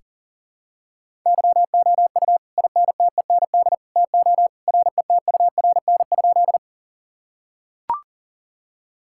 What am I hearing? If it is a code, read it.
Morse code, heard as a keyed tone, about 40 wpm, 700 Hz. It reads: YOU INTEND TO RETURN?
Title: Sentences from Top 1600 Words 40wpm